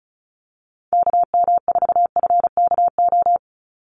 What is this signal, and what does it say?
Morse code, heard as a keyed tone, about 34 wpm, 700 Hz. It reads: KM4FKO
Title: USA Call Signs 35wpm